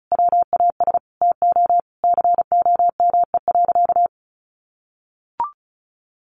Morse code, heard as a keyed tone, about 34 wpm, 700 Hz. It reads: WAS TO COME.
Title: Sentences from Top 700 Words 35wpm